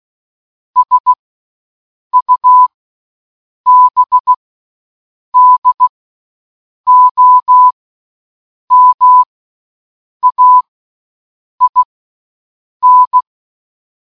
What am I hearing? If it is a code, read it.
Morse code, heard as a keyed tone, about 16 wpm, spaced as for 7 wpm, 1000 Hz. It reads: SUBDOMAIN
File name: morse-code-button